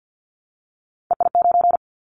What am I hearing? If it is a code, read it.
Morse code, heard as a keyed tone, about 50 wpm, 700 Hz. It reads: EI9
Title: Callsign Prefixes for Top 50 DXCC - 50wpm